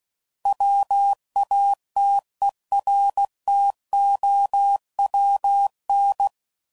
Morse code, heard as a keyed tone, about 16 wpm, 800 Hz. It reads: WATERTOWN
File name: CW Code Groups @15WPM